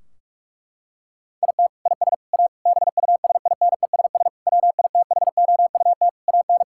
Morse code, heard as a keyed tone, about 42 wpm, 700 Hz. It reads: IT IS A BUSINESS WITHOUT AN
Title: Sentences from Top 1600 Words 45wpm